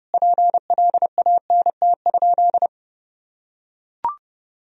Morse code, heard as a keyed tone, about 32 wpm, 700 Hz. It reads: PLANT?
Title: Sentences from Top 1000 Words - Review Part 1 - 30wpm